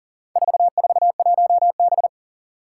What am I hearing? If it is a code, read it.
Morse code, heard as a keyed tone, about 40 wpm, 700 Hz. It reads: 441B
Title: Four Challenging Letter-Number Combinations 40wpm